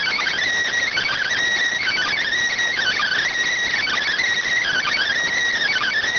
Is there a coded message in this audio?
Yes: Emergency Alert System (SAME) tones (header bursts) from 0.9 to 5.1 s, heard as harsh buzzing bursts.